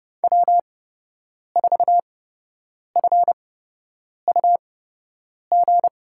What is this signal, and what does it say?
Morse code, heard as a keyed tone, about 30 wpm, spaced as for 8 wpm, 700 Hz. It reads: W4FUG
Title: USA Call Signs - 8x Character Spacing - 30wpm